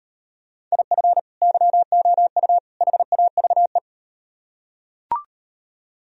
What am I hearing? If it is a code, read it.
Morse code, heard as a keyed tone, about 38 wpm, 700 Hz. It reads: IF YOU HAVE